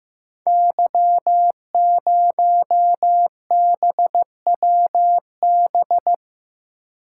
Morse code, heard as a keyed tone, about 15 wpm, 700 Hz. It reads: Y0BWB
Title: Code Groups - 5 Letter:Numbers - 15wpm